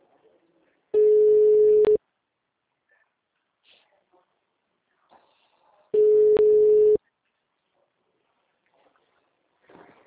Ringback tone (Europe, 425 Hz).